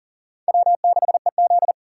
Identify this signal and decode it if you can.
Morse code, heard as a keyed tone, about 50 wpm, 700 Hz. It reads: W6EZ
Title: Speed Racing - USA 1x2 Call Signs 40wpm